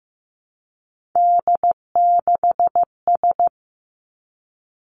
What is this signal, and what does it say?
Morse code, heard as a keyed tone, about 15 wpm, 700 Hz. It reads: D6S
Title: Three Challenging Letter-Number Combinations 15wpm